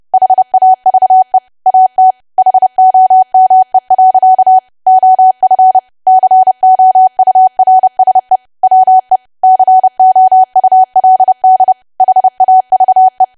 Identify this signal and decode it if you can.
Morse code, heard as a keyed tone, about 30 wpm, 750 Hz. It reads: HAVE AT HOME. OF COURSE WE COULD HAVE
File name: cwa / W1AW-Level3 / 30WPM